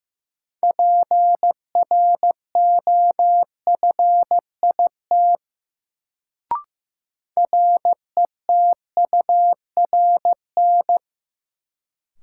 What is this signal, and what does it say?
Morse code, heard as a keyed tone, about 15 wpm, 700 Hz. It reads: PROFIT RETURN
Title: Sentences from Top 800 Words - New Word Drill - 15wpm